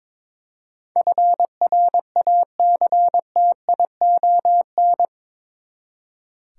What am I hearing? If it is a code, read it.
Morse code, heard as a keyed tone, about 22 wpm, 700 Hz. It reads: FRACTION
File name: practice / 22wpm / Eight Letter Words - 22wpm